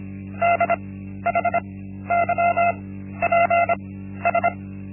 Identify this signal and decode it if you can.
Morse code, heard as a keyed tone, about 25 wpm, spaced as for 13 wpm, 690 Hz. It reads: DHYPS
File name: Linksmakalnis-1